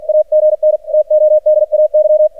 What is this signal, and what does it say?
Morse code, heard as a keyed tone, about 48 wpm, 610 Hz. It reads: AGN TODAY
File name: archived / CRSnet